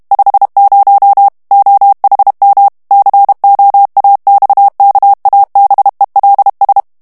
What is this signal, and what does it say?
Morse code, heard as a keyed tone, about 32 wpm, 800 Hz. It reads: 50 OHM COAXKABELS
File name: HOLLANDAIS texte clair 1 vitesse 30 mots minute